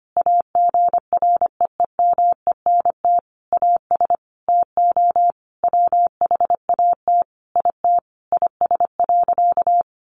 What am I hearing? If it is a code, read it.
Morse code, heard as a keyed tone, about 25 wpm, 700 Hz. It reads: AGREEMENT AS TO WHAT IT IS.
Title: Sentences from Top 1500 Words 25wpm